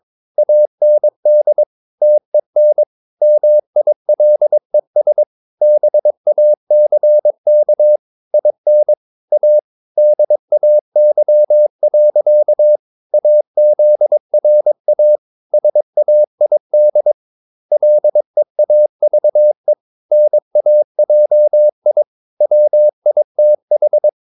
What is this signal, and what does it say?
Morse code, heard as a keyed tone, about 22 wpm, 600 Hz. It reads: AND TEN MILES BACK IN A DAY. AZRA SAID LEAVE NAJI WITH
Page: Romanen Never af Ken Follet på engelsk.